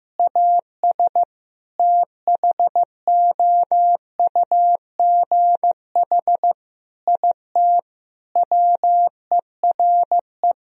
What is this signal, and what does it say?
Morse code, heard as a keyed tone, about 15 wpm, 700 Hz. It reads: AS THOUGH IT WERE
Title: Sentences from Top 1000 Words - Review Part 2 - 15wpm